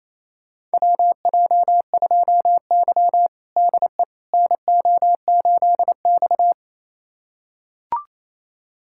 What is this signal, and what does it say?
Morse code, heard as a keyed tone, about 28 wpm, 700 Hz. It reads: WJ2Y DE NO8X
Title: USA Callsign DE Callsign - 4 Characters - 28wpm